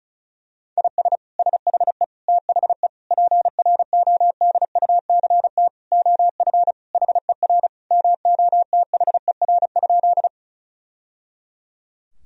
Morse code, heard as a keyed tone, about 34 wpm, 700 Hz. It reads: IS SHE THE PRODUCT OF HER MOTHER?
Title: Sentences from Top 1000 Words - Review Part 2 - 35wpm